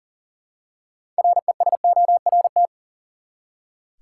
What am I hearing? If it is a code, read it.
Morse code, heard as a keyed tone, about 37 wpm, 700 Hz. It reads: RESORT
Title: Six Letter Words - 40wpm